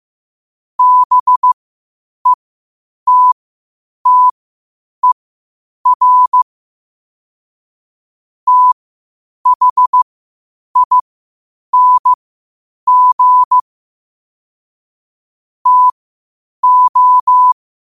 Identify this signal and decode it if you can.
Morse code, heard as a keyed tone, about 15 wpm, spaced as for 8 wpm, 1000 Hz. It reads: BETTER THING TO
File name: qotdfiles / 8wpm